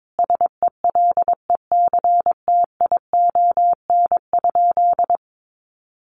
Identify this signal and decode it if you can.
Morse code, heard as a keyed tone, about 22 wpm, 700 Hz. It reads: SELECTION?
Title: Sentences from Top 1400 Words 22wpm